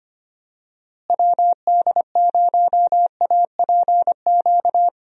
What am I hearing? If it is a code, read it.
Morse code, heard as a keyed tone, about 25 wpm, 700 Hz. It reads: WD0APQ
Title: USA Call Signs - 6 Characters - Repeated 3x - 25wpm